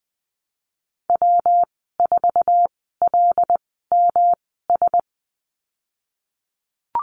Morse code, heard as a keyed tone, about 20 wpm, spaced as for 15 wpm, 700 Hz. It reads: W4LMS
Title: 167 Lesson 27g - Number 7 - Call Signs 20wpm